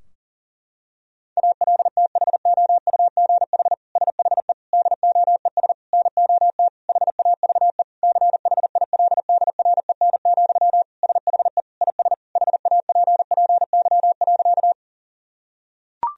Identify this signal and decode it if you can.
Morse code, heard as a keyed tone, about 37 wpm, 700 Hz. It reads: ALTHOUGH SHE DOES NOT HAVE CHILDREN, SHE IS HAPPY.
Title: Sentences from Top 1000 Words - Review Part 3 - 40wpm